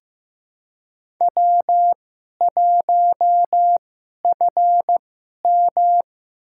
Morse code, heard as a keyed tone, about 15 wpm, spaced as for 11 wpm, 700 Hz. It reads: W1FM
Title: USA Call Signs - 2x Character Spacing - 15wpm